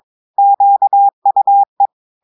Morse code, heard as a keyed tone, about 23 wpm, 800 Hz. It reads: QUE
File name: audios / portugais / portugais texte en clair 2 vitesse 22 mots minutes